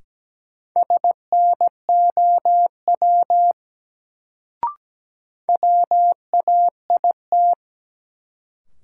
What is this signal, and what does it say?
Morse code, heard as a keyed tone, about 17 wpm, 700 Hz. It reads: SNOW WAIT
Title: Sentences from Top 500 Words - New Word Drill - 17wpm